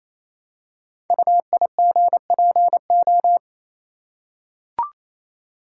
Morse code, heard as a keyed tone, about 27 wpm, 700 Hz. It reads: UIGPO
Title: Code Groups - 5 Letters - 28wpm